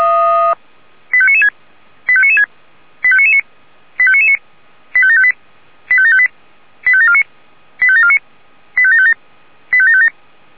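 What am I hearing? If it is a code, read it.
Five-tone selective call (ZVEI), heard as quick runs of five tones, tones of about 70 ms. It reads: 85996 85996 85999 85999 85758 85758 85749 85749 85757 85757
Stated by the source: ZVEI five-tone selective call received on SDR.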